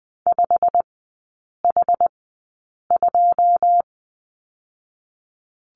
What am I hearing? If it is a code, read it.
Morse code, heard as a keyed tone, about 19 wpm, 700 Hz. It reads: 5 H 2
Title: 114 Lesson 19e - Letter w - Three Characters 20wpm